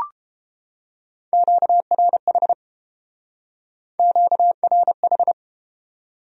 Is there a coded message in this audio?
Yes: Morse code from 1.3 to 5.3 s, heard as a keyed tone, about 30 wpm, 700 Hz. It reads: QRH QRH